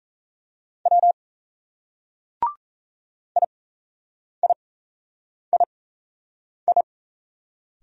Morse code, heard as a keyed tone, about 35 wpm, spaced as for 8 wpm, 700 Hz. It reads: W IIII